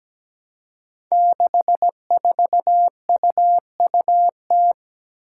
Morse code, heard as a keyed tone, about 17 wpm, 700 Hz. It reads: 64UUT
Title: Five Challenging Letter-Number Combinations 17wpm